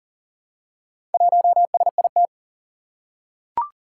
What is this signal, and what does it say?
Morse code, heard as a keyed tone, about 38 wpm, 700 Hz. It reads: 1SIT
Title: Four Challenging Letter-Number Combinations 40wpm